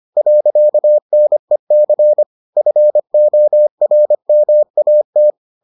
Morse code, heard as a keyed tone, about 25 wpm, 600 Hz. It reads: .NEC FORMAT